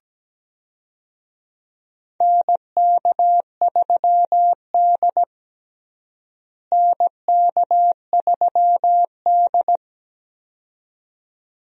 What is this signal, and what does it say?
Morse code, heard as a keyed tone, about 17 wpm, 700 Hz. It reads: NK3D NK3D
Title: USA Call Signs - 4 Characters - Repeated 2x - 17wpm